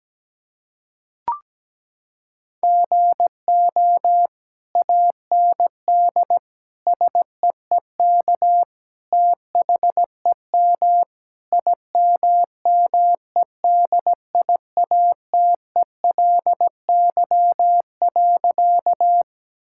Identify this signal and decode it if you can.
Morse code, heard as a keyed tone, about 17 wpm, 700 Hz. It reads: GO AND SEEK THEM IMMEDIATELY.
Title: Sentences from Top 1600 Words 17wpm